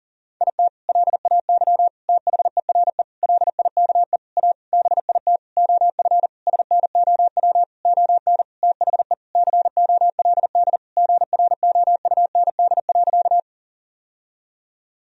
Morse code, heard as a keyed tone, about 40 wpm, 700 Hz. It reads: IT LAY THERE LIKE A BIT OF SNOW ON THE COLD GROUND.
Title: Sentences from Top 1000 Words - Review Part 3 - 40wpm